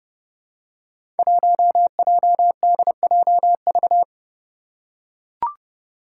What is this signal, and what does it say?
Morse code, heard as a keyed tone, about 30 wpm, 700 Hz. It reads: 1JDJV
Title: Five Challenging Letter-Number Combinations 30wpm